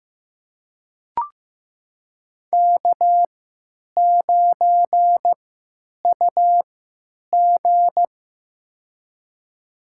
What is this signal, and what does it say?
Morse code, heard as a keyed tone, about 15 wpm, spaced as for 9 wpm, 700 Hz. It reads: K9UG
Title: USA Call Signs - 4 Characters - 3x Character Spacing - 15wpm